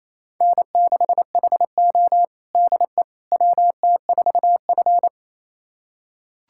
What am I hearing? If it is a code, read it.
Morse code, heard as a keyed tone, about 28 wpm, 700 Hz. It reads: N6HO DE WT4F
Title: USA Callsign DE Callsign - 4 Characters - 28wpm